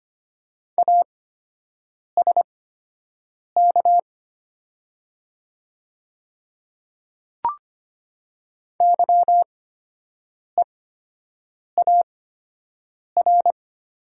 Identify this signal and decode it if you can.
Morse code, heard as a keyed tone, about 25 wpm, spaced as for 7 wpm, 700 Hz. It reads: ASK YEAR
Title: 8x Character Spacing - Top 300 Words - 25wpm